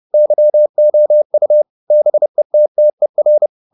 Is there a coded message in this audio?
Yes: Morse code from 0.1 to 3.5 s, heard as a keyed tone, about 30 wpm, 600 Hz. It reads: YOU BETTER